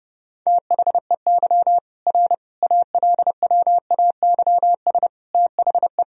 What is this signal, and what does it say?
Morse code, heard as a keyed tone, about 32 wpm, 700 Hz. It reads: THEY R ALWAYS THE
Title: Sentences from Top 300 Words and Abbreviations 30wpm